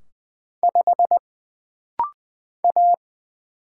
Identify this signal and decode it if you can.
Morse code, heard as a keyed tone, about 20 wpm, 700 Hz. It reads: 5 A
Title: Single Letter-Number Alt - Rapid-Fire 20wpm